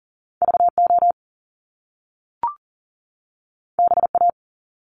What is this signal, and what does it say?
Morse code, heard as a keyed tone, about 42 wpm, 700 Hz. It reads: VO BA
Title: Top Two Letter Combinations in Words 40wpm